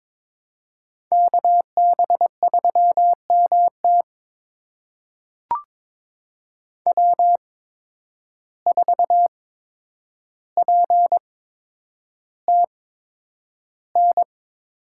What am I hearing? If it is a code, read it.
Morse code, heard as a keyed tone, about 22 wpm, spaced as for 9 wpm, 700 Hz. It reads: KB3MT W4PTN
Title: USA Call Signs - 5 Characters - 8x Character Spacing - 22wpm